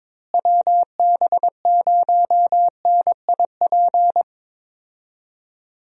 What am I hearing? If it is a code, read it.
Morse code, heard as a keyed tone, about 22 wpm, 700 Hz. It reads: WB0NIP
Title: USA Call Signs - 6 Characters - Repeated 2x - 22wpm